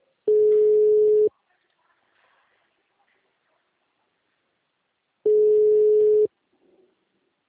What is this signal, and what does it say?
Ringback tone (Europe, 425 Hz).
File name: out- -2004- - - .75884